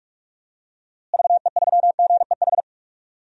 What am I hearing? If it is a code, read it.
Morse code, heard as a keyed tone, about 46 wpm, 700 Hz. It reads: VE3GEH
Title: DXCC-100 Call Signs - 6 Characters - Repeated 3x - 45wpm